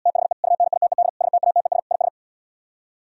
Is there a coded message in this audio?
Yes: Morse code from 0.1 to 2.1 s, heard as a keyed tone, about 77 wpm, 700 Hz. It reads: THE UNITED STATES IS